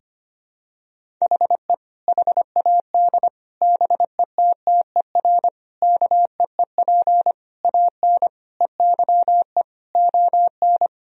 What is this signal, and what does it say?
Morse code, heard as a keyed tone, about 25 wpm, 700 Hz. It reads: HE HAD BETTER KEEP AN EYE ON
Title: Sentences from Top 1000 Words - Review Part 1 - 25wpm